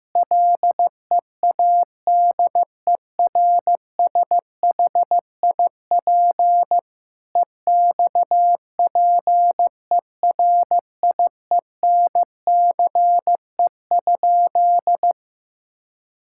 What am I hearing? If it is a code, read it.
Morse code, heard as a keyed tone, about 15 wpm, 700 Hz. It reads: LEADERSHIP EXPERIENCE?
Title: Sentences from Top 1600 Words 15wpm